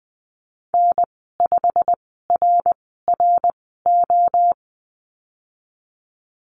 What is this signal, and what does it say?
Morse code, heard as a keyed tone, about 19 wpm, spaced as for 14 wpm, 700 Hz. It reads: N5RRO
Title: 173 Lesson 28g - Number 9 - Call Signs 20wpm